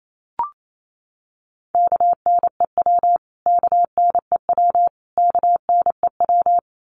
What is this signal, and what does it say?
Morse code, heard as a keyed tone, about 28 wpm, 700 Hz. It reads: KNEW KNEW KNEW
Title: Top 500 Words Repeated 28wpm